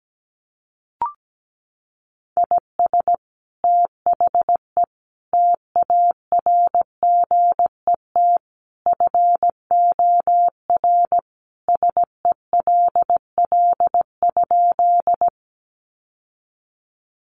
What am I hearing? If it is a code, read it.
Morse code, heard as a keyed tone, about 17 wpm, 700 Hz. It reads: IS THE TARGET FOR SELL?
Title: Sentences from Top 1200 Words 17wpm